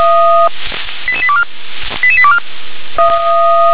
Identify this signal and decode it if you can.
Five-tone selective call (ZVEI), heard as quick runs of five tones, tones of about 70 ms. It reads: 88824 88824